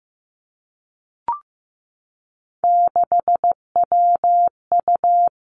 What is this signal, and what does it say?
Morse code, heard as a keyed tone, about 15 wpm, 700 Hz. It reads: 6WU
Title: Three Challenging Letter-Number Combinations 15wpm